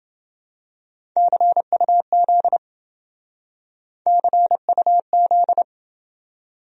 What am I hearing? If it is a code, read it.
Morse code, heard as a keyed tone, about 27 wpm, 700 Hz. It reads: CUZ CUZ